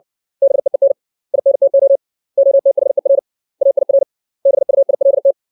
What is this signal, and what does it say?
Morse code, heard as a keyed tone, about 60 wpm, 550 Hz. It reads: BEEN INTO OTHER AIR BUILT
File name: 632 words / 632 Words 60WPM